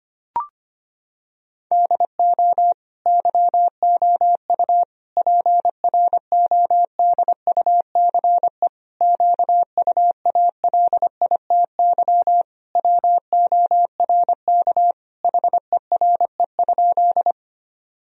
Morse code, heard as a keyed tone, about 25 wpm, 700 Hz. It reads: DO YOU PRODUCE QUALITY WORK HERE?
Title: Sentences from Top 1000 Words - Review Part 4 - 25wpm